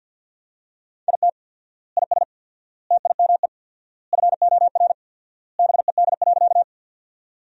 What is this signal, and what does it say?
Morse code, heard as a keyed tone, about 50 wpm, 700 Hz. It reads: IT IS TIME FOR BED.
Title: 4x Word-Spacing - Sentences from Top 500 Words - 50wpm